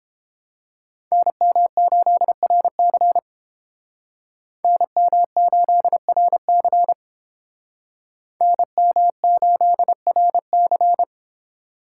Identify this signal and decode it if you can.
Morse code, heard as a keyed tone, about 30 wpm, 700 Hz. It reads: NM8RC NM8RC NM8RC